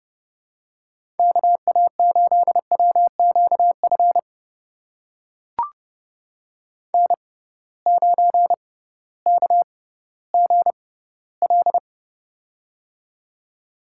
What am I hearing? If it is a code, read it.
Morse code, heard as a keyed tone, about 30 wpm, spaced as for 16 wpm, 700 Hz. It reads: KA8WQF N9KGL
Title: USA Call Signs - 6x Character Spacing - 30wpm